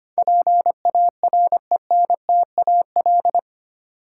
Morse code, heard as a keyed tone, about 25 wpm, 700 Hz. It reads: PARENTAL
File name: Eight Letter Words - 25wpm